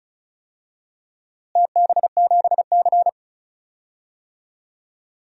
Morse code, heard as a keyed tone, about 36 wpm, 700 Hz. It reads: TB7C
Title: DXCC-100 Call Signs - 4 Characters - Repeated 3x - 35wpm